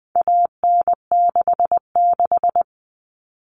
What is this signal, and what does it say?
Morse code, heard as a keyed tone, about 20 wpm, 700 Hz. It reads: AN66
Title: All Cut Numbers - 4-Digit 20wpm